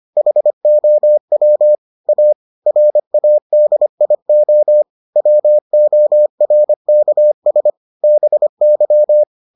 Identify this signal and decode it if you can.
Morse code, heard as a keyed tone, about 25 wpm, 600 Hz. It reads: HOW A RADIO WORKS BY